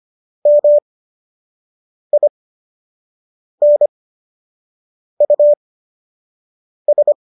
Morse code, heard as a keyed tone, about 25 wpm, spaced as for 6 wpm, 590 Hz. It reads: MINUS
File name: qso012 06